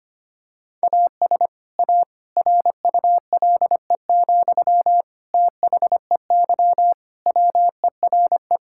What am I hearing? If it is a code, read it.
Morse code, heard as a keyed tone, about 25 wpm, 700 Hz. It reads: AS A RULE, THEY WERE
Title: Sentences from Top 1000 Words - Review Part 6 - 25wpm